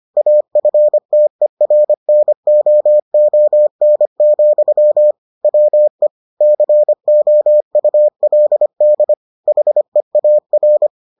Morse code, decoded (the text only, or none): AFTERNOON, WE COULD HEAR